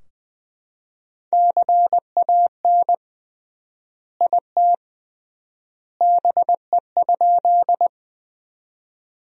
Morse code, heard as a keyed tone, about 20 wpm, 700 Hz. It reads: CAN IT BE?
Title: 3x Word-Spacing - Sentences from Top 100 Words - 20wpm